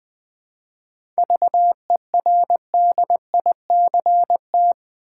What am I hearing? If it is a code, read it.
Morse code, heard as a keyed tone, about 20 wpm, 700 Hz. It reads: VERDICT
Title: 7 Letter Words - First 3 Letters Given - 20wpm